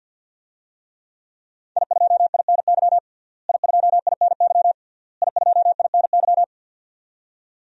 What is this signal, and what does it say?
Morse code, heard as a keyed tone, about 51 wpm, 700 Hz. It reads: I2INY I2INY I2INY
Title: DXCC-100 Call Signs - 5 Characters - Repeated 3x - 50wpm